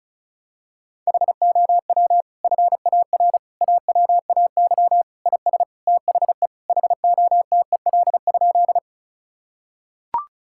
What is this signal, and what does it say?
Morse code, heard as a keyed tone, about 34 wpm, 700 Hz. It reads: HOW FAR AWAY IS THE HOTEL?
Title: Sentences from Top 1600 Words 35wpm